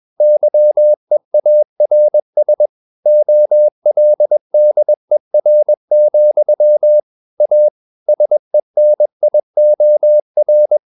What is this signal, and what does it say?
Morse code, heard as a keyed tone, about 21 wpm, 600 Hz. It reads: YEARS OLDER, A SENIOR
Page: CW med Gnister